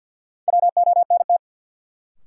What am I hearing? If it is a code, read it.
Morse code, heard as a keyed tone, about 47 wpm, 700 Hz. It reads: WONT
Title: Sentences from Top 1000 Words - Review Part 6 - New Word Drill - 50wpm